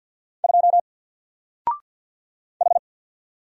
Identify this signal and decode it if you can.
Morse code, heard as a keyed tone, about 49 wpm, 700 Hz. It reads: 2 H
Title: Single Character (Letter:Number:Punctuation:Prosign) 50wpm